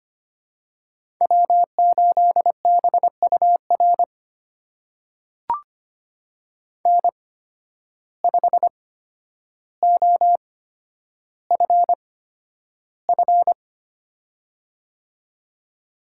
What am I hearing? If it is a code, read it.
Morse code, heard as a keyed tone, about 25 wpm, spaced as for 11 wpm, 700 Hz. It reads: W8BUR N5OFF
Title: USA Call Signs - 5 Characters - 8x Character Spacing - 25wpm